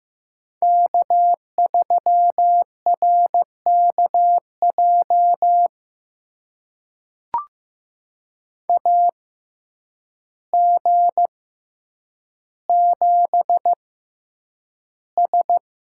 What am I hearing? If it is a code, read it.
Morse code, heard as a keyed tone, about 15 wpm, spaced as for 8 wpm, 700 Hz. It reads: K3RKJ AG7S
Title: USA Call Signs - 5 Characters - 6x Character Spacing - 15wpm